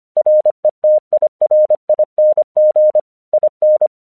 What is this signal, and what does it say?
Morse code, heard as a keyed tone, about 25 wpm, 610 Hz. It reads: RETIRING IN
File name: 1 English 25wpm / AQC13